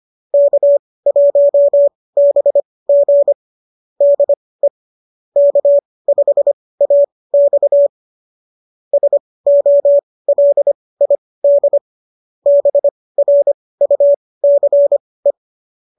Morse code, heard as a keyed tone, about 25 wpm, spaced as for 18 wpm, 590 Hz. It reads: K1BG DE K5AX SOLID BRUCE
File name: qso201 18